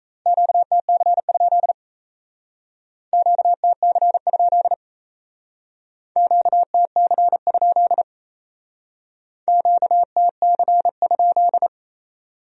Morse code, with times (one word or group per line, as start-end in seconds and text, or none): 0.3-1.7 QTC?
3.1-4.7 QTC?
6.2-8.0 QTC?
9.5-11.7 QTC?